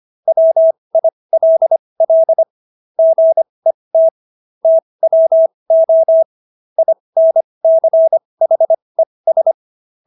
Morse code, heard as a keyed tone, about 25 wpm, spaced as for 20 wpm, 650 Hz. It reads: WILL GET TWO INCHES